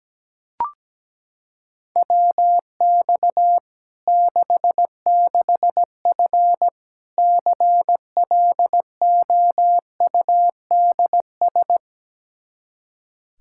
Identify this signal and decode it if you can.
Morse code, heard as a keyed tone, about 17 wpm, 700 Hz. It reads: WX 66F CLOUDS
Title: Practice QSO - Exchange 2 - Info - 17wpm